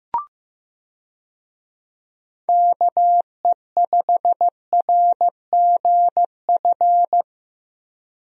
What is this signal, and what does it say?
Morse code, heard as a keyed tone, about 16 wpm, 700 Hz. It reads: KE5RGF
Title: USA Call Signs - 6 Characters - Repeated 2x - 15wpm